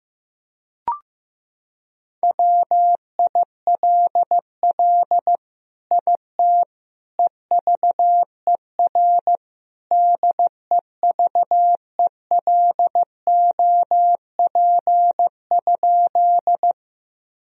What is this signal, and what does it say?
Morse code, heard as a keyed tone, about 15 wpm, 700 Hz. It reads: WILL IT EVER DEVELOP?
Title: Sentences from Top 1000 Words - Review Part 2 - 15wpm